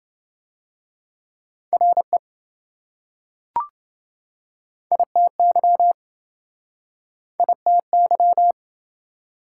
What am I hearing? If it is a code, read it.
Morse code, heard as a keyed tone, about 27 wpm, 700 Hz. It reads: RE ITY ITY